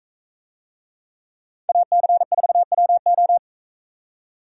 Morse code, heard as a keyed tone, about 42 wpm, 700 Hz. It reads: AC4WO